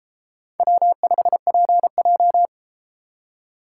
Morse code, heard as a keyed tone, about 31 wpm, 700 Hz. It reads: W5PJ